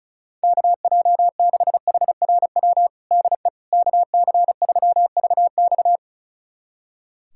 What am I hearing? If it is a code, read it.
Morse code, heard as a keyed tone, about 36 wpm, 700 Hz. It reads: KJ6HRW DE KC3VX
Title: USA Callsign DE Callsign 35wpm